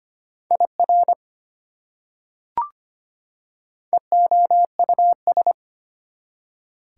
Morse code, heard as a keyed tone, about 25 wpm, 700 Hz. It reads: IR EOUS
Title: Prefixes and Suffixes 25wpm